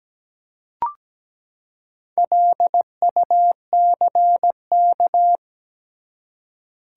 Morse code, heard as a keyed tone, about 17 wpm, 700 Hz. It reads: LUCK
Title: Four Letter Words - 17wpm